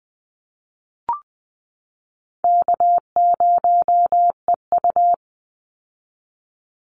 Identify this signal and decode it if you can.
Morse code, heard as a keyed tone, about 20 wpm, 700 Hz. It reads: K0EU
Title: Easy Call Signs Repeated 2x - 20wpm